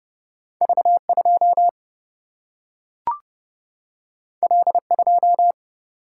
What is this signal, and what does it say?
Morse code, heard as a keyed tone, about 30 wpm, 700 Hz. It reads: V2 L2
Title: Single-Letter Single-Number 30wpm